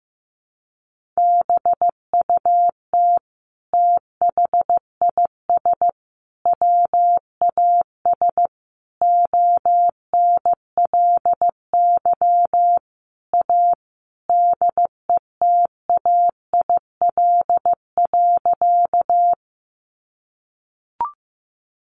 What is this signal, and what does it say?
Morse code, heard as a keyed tone, about 15 wpm, 700 Hz. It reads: BUT THIS WAS ONLY A DETAIL.
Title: Sentences from Top 1500 Words 15wpm